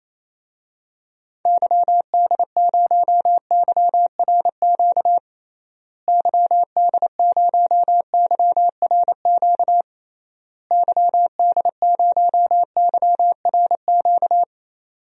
Morse code, heard as a keyed tone, about 28 wpm, 700 Hz. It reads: YD0YRQ YD0YRQ YD0YRQ
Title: DXCC-100 Call Signs - 6 Characters - Repeated 3x - 28wpm